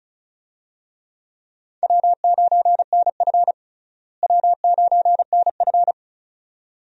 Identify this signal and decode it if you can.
Morse code, heard as a keyed tone, about 36 wpm, 700 Hz. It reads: W9NF W9NF
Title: USA Call Signs - 4 Characters - Repeated 2x - 35wpm